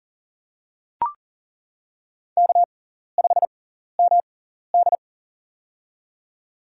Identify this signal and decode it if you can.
Morse code, heard as a keyed tone, about 40 wpm, spaced as for 14 wpm, 700 Hz. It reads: K5MD
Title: USA Call Signs - 4 Characters - 6x Character Spacing - 40wpm